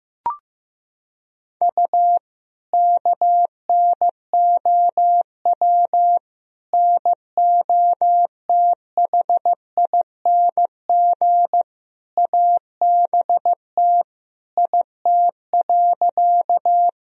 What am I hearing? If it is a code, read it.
Morse code, heard as a keyed tone, about 15 wpm, 700 Hz. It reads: U KNOW NOTHING ABT IT.
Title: Sentences from Top 500 Words and Abbreviations 15wpm